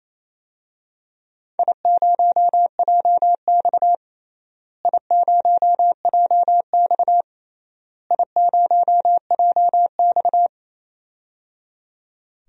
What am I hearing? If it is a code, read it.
Morse code, heard as a keyed tone, about 28 wpm, 700 Hz. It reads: I0JX I0JX I0JX
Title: DXCC-100 Call Signs - 4 Characters - Repeated 3x - 28wpm